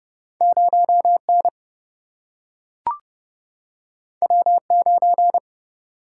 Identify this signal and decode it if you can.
Morse code, heard as a keyed tone, about 29 wpm, 700 Hz. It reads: 0N W9
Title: Single-Letter Single-Number 30wpm